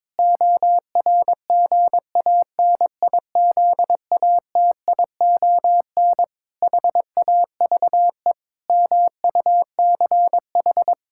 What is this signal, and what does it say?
Morse code, heard as a keyed tone, about 22 wpm, 700 Hz. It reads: ORGANIZATION HAVE MUCH
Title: Sentences from Top 1000 Words - Review Part 5 - 22wpm